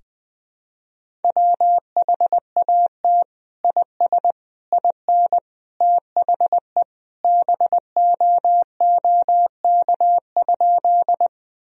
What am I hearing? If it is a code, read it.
Morse code, heard as a keyed tone, about 21 wpm, 700 Hz. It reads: WHAT IS IN THE BOOK?
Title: Sentences from Top 1000 Words - Review Part 1 - 20wpm